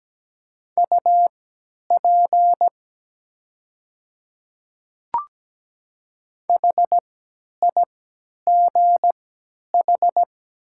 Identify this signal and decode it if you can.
Morse code, heard as a keyed tone, about 17 wpm, spaced as for 10 wpm, 700 Hz. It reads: UP HIGH
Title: 3x Character Spacing - Top 200 Words - 17wpm